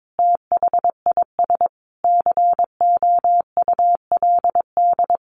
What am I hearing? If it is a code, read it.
Morse code, heard as a keyed tone, about 22 wpm, 700 Hz. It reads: THIS COULD
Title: Sentences from Top 1400 Words 22wpm